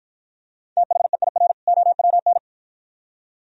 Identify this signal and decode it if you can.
Morse code, heard as a keyed tone, about 48 wpm, 700 Hz. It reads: THEIR OWN